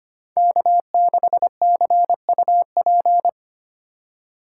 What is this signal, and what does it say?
Morse code, heard as a keyed tone, about 30 wpm, 700 Hz. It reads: K6CUP
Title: Speed Racing - USA 1x3 Call Signs 25wpm